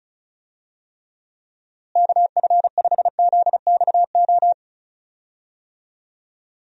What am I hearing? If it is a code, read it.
Morse code, heard as a keyed tone, about 36 wpm, 700 Hz. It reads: KF5ZXO
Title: USA Call Signs - 6 Characters - Repeated 3x - 35wpm